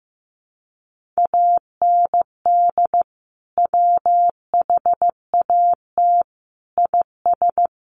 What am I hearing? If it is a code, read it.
Morse code, heard as a keyed tone, about 15 wpm, 700 Hz. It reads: AND WHAT IS
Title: Sentences from Top 1500 Words 15wpm